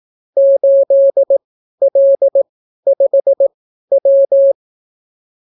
Morse code, heard as a keyed tone, about 18 wpm, spaced as for 12 wpm, 550 Hz. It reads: 8L5W